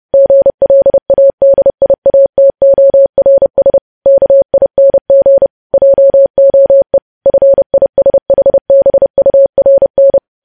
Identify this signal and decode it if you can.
Morse code, heard as a keyed tone, about 30 wpm, 550 Hz. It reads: GLADIATORS KING JOE FISHBURN
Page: news headlines in morse code at 30 wpm